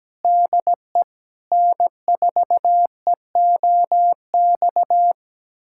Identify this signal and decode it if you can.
Morse code, heard as a keyed tone, about 17 wpm, 700 Hz. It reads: DE N4EOX
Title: USA Callsign DE Callsign - 5 Characters - 17wpm